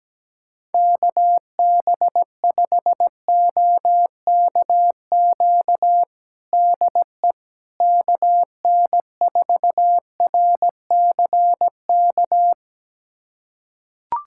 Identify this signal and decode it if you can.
Morse code, heard as a keyed tone, about 17 wpm, 700 Hz. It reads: KB5OKQ DE KN4RCK
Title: USA Callsign DE Callsign 17wpm